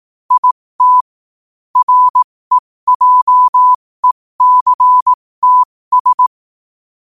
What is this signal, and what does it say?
Morse code, heard as a keyed tone, about 18 wpm, spaced as for 15 wpm, 1000 Hz. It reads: IT REJECTS
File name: qotdfiles / 15wpm